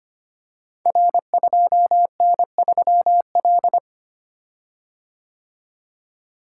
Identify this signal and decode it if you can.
Morse code, heard as a keyed tone, about 25 wpm, 700 Hz. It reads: R2N3L
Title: Code Groups - 5 Letter:Numbers - 25wpm